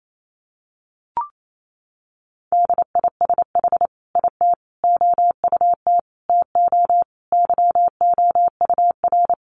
Morse code, heard as a keyed tone, about 28 wpm, 700 Hz. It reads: DISH IT OUT TO YOUR
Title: Sentences from Top 1300 Words 28wpm